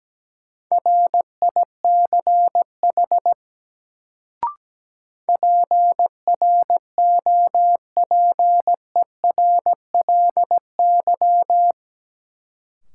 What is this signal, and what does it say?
Morse code, heard as a keyed tone, about 17 wpm, 700 Hz. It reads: RICH PROPERLY
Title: Sentences from Top 1100 Words - New Word Drill - 17wpm